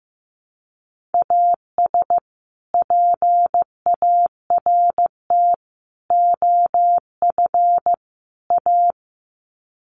Morse code, heard as a keyed tone, about 14 wpm, 700 Hz. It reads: AS PART OF A
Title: Sets of 4 Words 15wpm